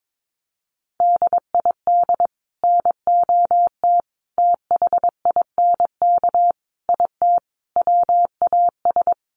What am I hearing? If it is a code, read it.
Morse code, heard as a keyed tone, about 22 wpm, 700 Hz. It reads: DID NOT THINK IT WAS
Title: Sets of 5 Words Encore 22wpm